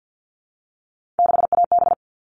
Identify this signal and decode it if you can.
Morse code, heard as a keyed tone, about 48 wpm, 700 Hz. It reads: 6AB
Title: All Cut Numbers - 3-Digit 50wpm